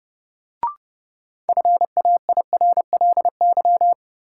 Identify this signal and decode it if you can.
Morse code, heard as a keyed tone, about 30 wpm, 700 Hz. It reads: FAIRLY
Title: Sentences from Top 1100 Words - New Word Drill - 30wpm